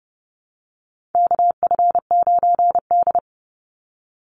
Morse code, heard as a keyed tone, about 30 wpm, 700 Hz. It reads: KF9D
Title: Call Signs Repeated 2x - 30wpm